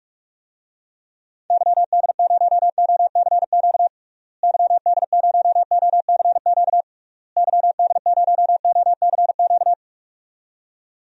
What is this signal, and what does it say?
Morse code, heard as a keyed tone, about 44 wpm, 700 Hz. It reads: YD0OCQ YD0OCQ YD0OCQ
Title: DXCC-100 Call Signs - 6 Characters - Repeated 3x - 45wpm